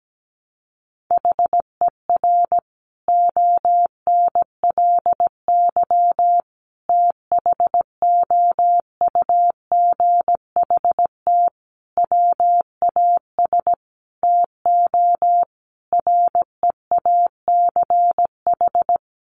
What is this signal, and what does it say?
Morse code, heard as a keyed tone, about 17 wpm, 700 Hz. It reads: HER ONLY THOUGHT WAS TO REACH
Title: Sentences from Top 400 Words 17wpm